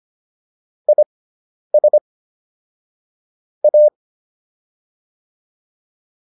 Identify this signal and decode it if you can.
Morse code, heard as a keyed tone, about 25 wpm, spaced as for 10 wpm, 600 Hz. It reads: IS A